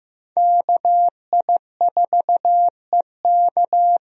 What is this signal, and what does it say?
Morse code, heard as a keyed tone, about 15 wpm, 700 Hz. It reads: KI4EK
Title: USA Call Signs - 5 Characters - Repeated 3x - 15wpm